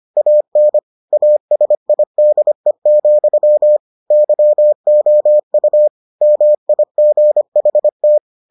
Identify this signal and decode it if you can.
Morse code, heard as a keyed tone, about 25 wpm, 600 Hz. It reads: AN ASIDE, YOU MIGHT